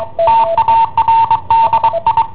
Morse code, heard as a keyed tone, about 23 wpm, 910 Hz. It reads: TARDI
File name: i7ohp / cassetta i7ohp B